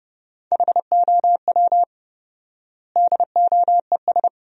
Morse code, heard as a keyed tone, about 30 wpm, 700 Hz. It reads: HOW DOES
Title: 4x Word-Spacing - Sentences from Top 400 Words - 30wpm